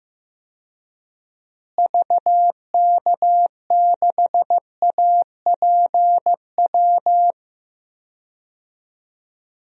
Morse code, heard as a keyed tone, about 15 wpm, 700 Hz. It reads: VK6APW
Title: DXCC-100 Call Signs - 6 Characters - Repeated 3x - 15wpm